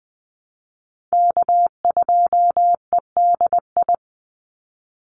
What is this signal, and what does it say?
Morse code, heard as a keyed tone, about 20 wpm, 700 Hz. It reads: K2EDI
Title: USA Call Signs 20wpm